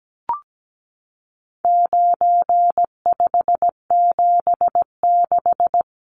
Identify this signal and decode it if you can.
Morse code, heard as a keyed tone, about 17 wpm, 700 Hz. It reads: 9576
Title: Numbers - 4-Digit 17wpm